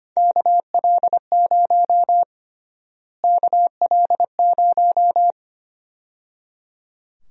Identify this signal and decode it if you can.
Morse code, heard as a keyed tone, about 25 wpm, 700 Hz. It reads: KL0 KL0
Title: USA Call Sign Prefixes - Repeated 2x - 25wpm